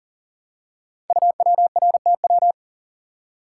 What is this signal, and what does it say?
Morse code, heard as a keyed tone, about 39 wpm, 700 Hz. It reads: UWRTW
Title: Code Groups - 5 Letters - 40wpm